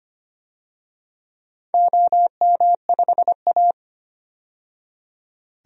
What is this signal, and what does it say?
Morse code, heard as a keyed tone, about 25 wpm, 700 Hz. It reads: OM5A
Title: DXCC-100 Call Signs - 4 Characters - 25wpm